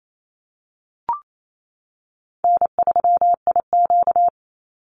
Morse code, heard as a keyed tone, about 28 wpm, 700 Hz. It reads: N3IQ
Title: Call Signs - 28wpm